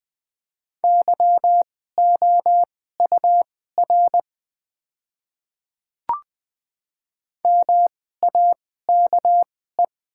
Morse code, heard as a keyed tone, about 20 wpm, spaced as for 15 wpm, 700 Hz. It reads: YOUR MAKE
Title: 2x Character Spacing - Top 100 Words - 20wpm